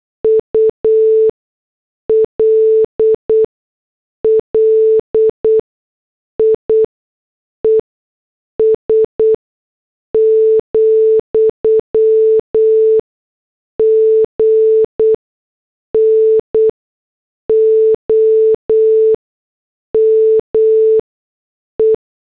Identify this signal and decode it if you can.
Morse code, heard as a keyed tone, about 8 wpm, spaced as for 6 wpm, 440 Hz. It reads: ULLIES, GNOME
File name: morse / headlines